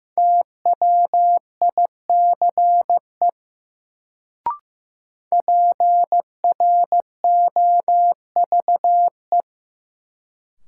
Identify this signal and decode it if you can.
Morse code, heard as a keyed tone, about 15 wpm, 700 Hz. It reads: TWICE PROVE
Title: Sentences from Top 1600 Words - New Word Drill - 15wpm